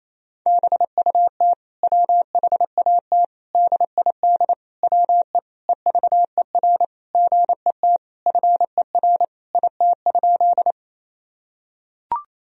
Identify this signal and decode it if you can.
Morse code, heard as a keyed tone, about 28 wpm, 700 Hz. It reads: BUT WHAT DID WE EVER GET FER IT?
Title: Sentences from Top 400 Words and Abbreviations 28wpm